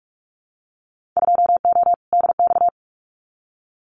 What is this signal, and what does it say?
Morse code, heard as a keyed tone, about 44 wpm, 700 Hz. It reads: 2O DX
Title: Winter Field Day Exchanges Encore 45wpm